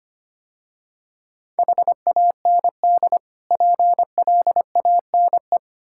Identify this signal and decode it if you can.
Morse code, heard as a keyed tone, about 25 wpm, 700 Hz. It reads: HAND PLANE
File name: practice / 25wpm / Difficult Words - Speed-Racing with Hints - 25wpm